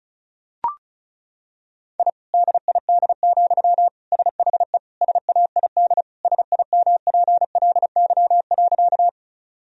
Morse code, heard as a keyed tone, about 34 wpm, 700 Hz. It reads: I DID, SHE SAID SIMPLY.
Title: Sentences from Top 1000 Words - Review Part 4 - 35wpm